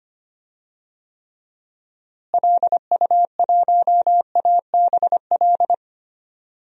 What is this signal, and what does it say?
Morse code, heard as a keyed tone, about 25 wpm, 700 Hz. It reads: LU1ABL
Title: DXCC-100 Call Signs - 6 Characters - Repeated 3x - 25wpm